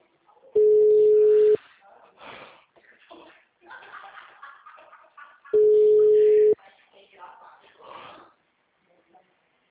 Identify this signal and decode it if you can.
Ringback tone (Europe, 425 Hz), with speech.
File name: out- -2001- - - .246